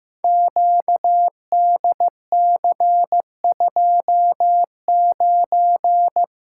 Morse code, heard as a keyed tone, about 15 wpm, 700 Hz. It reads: QDC29
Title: Code Groups - 5 Letter:Numbers - 15wpm